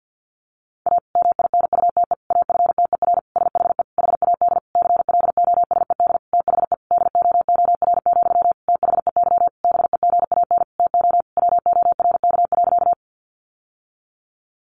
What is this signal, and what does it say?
Morse code, heard as a keyed tone, about 50 wpm, 700 Hz. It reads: A MINUTE AFTER SHE HAD CLOSED THE DOOR, THEY BEGAN TO WORK.
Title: Sentences from Top 1400 Words 50wpm